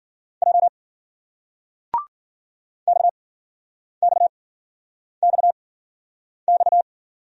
Morse code, heard as a keyed tone, about 56 wpm, spaced as for 9 wpm, 700 Hz. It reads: J XXXX